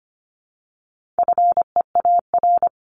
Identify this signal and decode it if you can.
Morse code, heard as a keyed tone, about 25 wpm, 700 Hz. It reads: FEAR
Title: Top 1751-2000 Words 25wpm